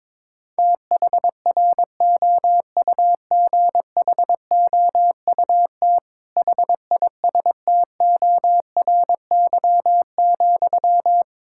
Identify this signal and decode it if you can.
Morse code, heard as a keyed tone, about 22 wpm, 700 Hz. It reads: THROUGHOUT HISTORY,
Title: Sentences from Top 1000 Words - Review Part 5 - 22wpm